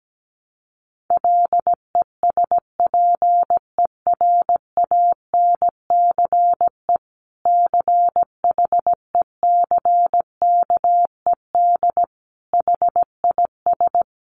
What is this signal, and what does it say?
Morse code, heard as a keyed tone, about 17 wpm, 700 Hz. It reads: LESPERANCE CHECKED HIS
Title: A Sound Of Thunder 17wpm